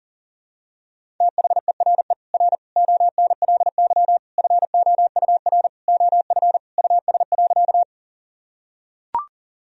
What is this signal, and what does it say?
Morse code, heard as a keyed tone, about 40 wpm, 700 Hz. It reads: THERE R ONLY FOUR OF US.
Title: Sentences from Top 300 Words and Abbreviations 40wpm